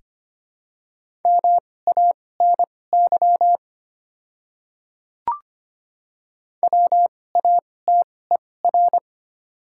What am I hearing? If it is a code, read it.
Morse code, heard as a keyed tone, about 25 wpm, spaced as for 18 wpm, 700 Hz. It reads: MANY WATER
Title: 2x Character Spacing - Top 200 Words - 25wpm